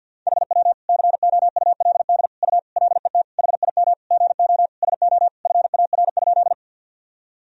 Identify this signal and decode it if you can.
Morse code, heard as a keyed tone, about 51 wpm, 700 Hz. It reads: HW COULD U LET HIM GO SO FAR?
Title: Sentences from Top 300 Words and Abbreviations 50wpm